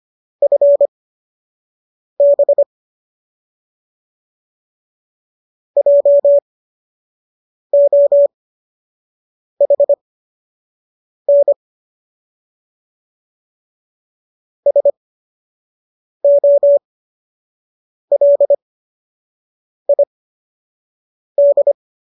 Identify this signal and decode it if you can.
Morse code, heard as a keyed tone, about 25 wpm, spaced as for 6 wpm, 570 Hz. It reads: FB JOHN SOLID